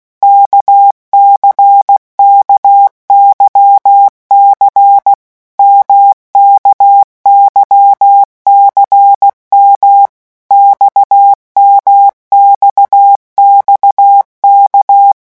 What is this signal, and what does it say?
Morse code, heard as a keyed tone, about 16 wpm, 800 Hz. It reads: KCKYC MKYCM XMXXK
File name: morse mp3 / revision1-vitesse 15 mots